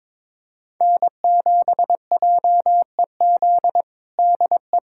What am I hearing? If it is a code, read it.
Morse code, heard as a keyed tone, about 21 wpm, 700 Hz. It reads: N7JEZ DE
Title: USA Callsign DE Callsign - 5 Characters - 22wpm